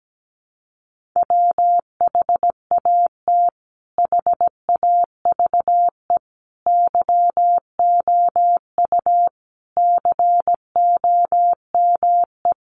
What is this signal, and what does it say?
Morse code, heard as a keyed tone, about 17 wpm, 700 Hz. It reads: WHAT HAVE YOU COME
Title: Sentences from Top 100 Words 17wpm